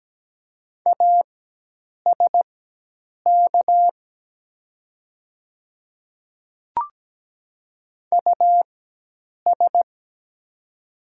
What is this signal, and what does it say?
Morse code, heard as a keyed tone, about 17 wpm, spaced as for 8 wpm, 700 Hz. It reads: ASK US
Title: 4x Character Spacing - Top 200 Words - 17wpm